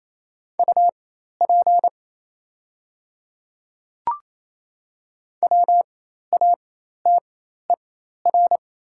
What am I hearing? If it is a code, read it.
Morse code, heard as a keyed tone, about 28 wpm, spaced as for 13 wpm, 700 Hz. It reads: UP WATER
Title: 4x Character Spacing - Top 300 Words - 28wpm